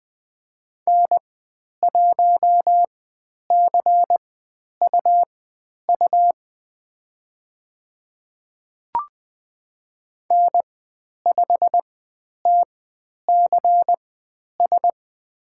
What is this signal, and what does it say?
Morse code, heard as a keyed tone, about 20 wpm, spaced as for 10 wpm, 700 Hz. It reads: N1CUU N5TCS
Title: 102 Lesson 17g - Letter u - Call Signs 20wpm